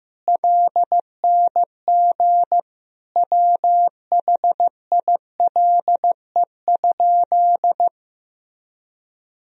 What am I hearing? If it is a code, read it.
Morse code, heard as a keyed tone, about 15 wpm, 700 Hz. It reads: LNG WHILE?
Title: Sentences from Top 300 Words and Abbreviations 15wpm